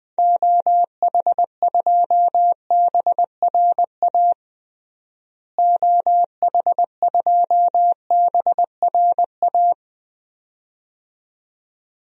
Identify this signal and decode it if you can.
Morse code, heard as a keyed tone, about 20 wpm, 700 Hz. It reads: OH2BRA OH2BRA
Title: DXCC-100 Call Signs - 6 Characters - Repeated 2x - 20wpm